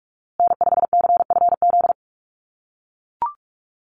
Morse code, heard as a keyed tone, about 44 wpm, 700 Hz. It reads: N5CFZ
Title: USA Call Signs Repeated 2x 45wpm